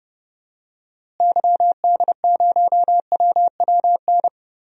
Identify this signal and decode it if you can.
Morse code, heard as a keyed tone, about 29 wpm, 700 Hz. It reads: YD0WWN
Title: DXCC-100 Call Signs - 6 Characters - Repeated 3x - 30wpm